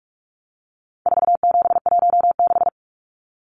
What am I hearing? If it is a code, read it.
Morse code, heard as a keyed tone, about 44 wpm, 700 Hz. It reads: 4716
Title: Numbers - 4-Digit 45wpm